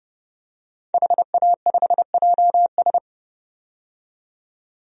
Morse code, heard as a keyed tone, about 30 wpm, 700 Hz. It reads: HA5JS
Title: DXCC-100 Call Signs - 5 Characters - 30wpm